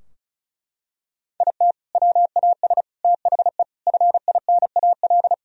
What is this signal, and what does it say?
Morse code, heard as a keyed tone, about 36 wpm, 700 Hz. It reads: IT WAS THE FINAL
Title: Sentences from Top 1000 Words - Review Part 2 - 35wpm